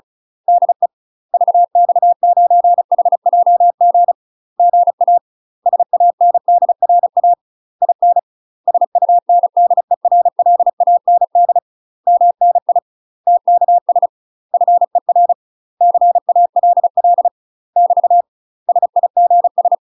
Morse code, heard as a keyed tone, about 36 wpm, 700 Hz. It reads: DE VX9HJG GA SANDRA IN SUNDERLAND MNI TKS FER CALL <BT> SIGS
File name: UK MASTER QSOs TRAINING / 35WPM / UK MASTER QSOs 0026